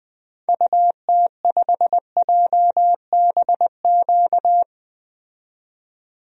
Morse code, heard as a keyed tone, about 20 wpm, 700 Hz. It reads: UT5JBQ
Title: DXCC-100 Call Signs - 6 Characters - 20wpm